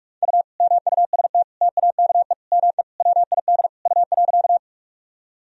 Morse code, heard as a keyed tone, about 46 wpm, 700 Hz. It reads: U MUST TAKE ME WID U.
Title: Sentences from Top 200 Words and Abbreviations 45wpm